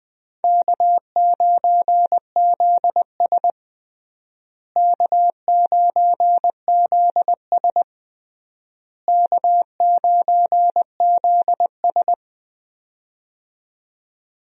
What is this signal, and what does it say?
Morse code, heard as a keyed tone, about 20 wpm, 700 Hz. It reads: K9ZS K9ZS K9ZS
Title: USA Call Signs - 4 Characters - Repeated 3x - 20wpm